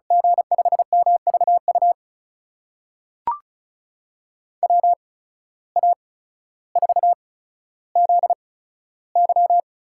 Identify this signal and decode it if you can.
Morse code, heard as a keyed tone, about 36 wpm, spaced as for 15 wpm, 700 Hz. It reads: G5MVU WA4ZY
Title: USA Call Signs - 6 Characters - 8x Character Spacing - 35wpm